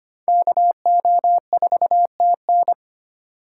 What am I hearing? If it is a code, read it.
Morse code, heard as a keyed tone, about 25 wpm, 700 Hz. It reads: KO4TN
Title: USA Call Signs - 5 Characters - 2x Character Spacing - 25wpm